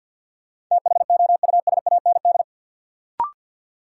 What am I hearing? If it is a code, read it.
Morse code, heard as a keyed tone, about 53 wpm, 700 Hz. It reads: THOUSAND
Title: Sentences from Top 600 Words - New Word Drill - 50wpm